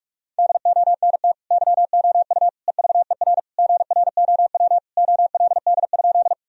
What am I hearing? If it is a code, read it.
Morse code, heard as a keyed tone, about 46 wpm, 700 Hz. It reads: DONT YOU EVER GROW OLD?
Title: Sentences from Top 1000 Words - Review Part 1 - 45wpm